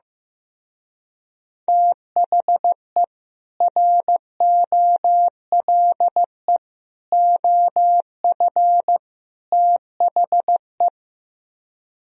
Morse code, heard as a keyed tone, about 15 wpm, 700 Hz. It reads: THE ROLE OF THE
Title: Speed-Racing - Sets of 4 Words 15wpm